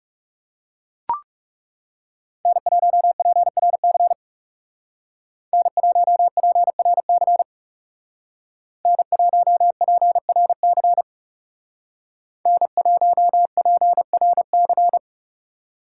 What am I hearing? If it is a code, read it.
Morse code, heard as a keyed tone, about 37 wpm, 700 Hz. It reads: N1PRC N1PRC N1PRC N1PRC
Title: Speed Racing - USA Call Signs 30wpm